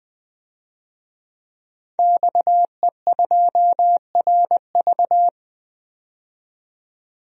Morse code, heard as a keyed tone, about 20 wpm, 700 Hz. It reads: XE2RV
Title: DXCC-100 Call Signs - 5 Characters - 20wpm